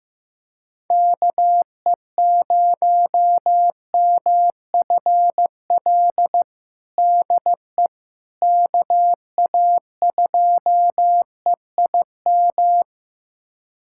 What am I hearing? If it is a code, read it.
Morse code, heard as a keyed tone, about 15 wpm, 700 Hz. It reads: KE0MFL DE KA2EIM
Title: USA Callsign DE Callsign - 6 Characters - 15wpm